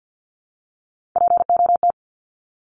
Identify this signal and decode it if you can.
Morse code, heard as a keyed tone, about 50 wpm, 700 Hz. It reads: POT
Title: Top 751-1000 Words 50wpm